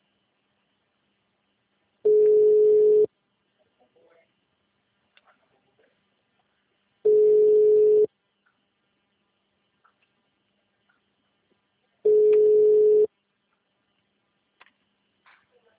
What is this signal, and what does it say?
Ringback tone (Europe, 425 Hz), with speech.